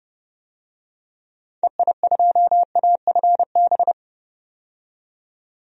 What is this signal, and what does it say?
Morse code, heard as a keyed tone, about 30 wpm, 700 Hz. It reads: EI2AFB
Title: DXCC-100 Call Signs - 6 Characters - 30wpm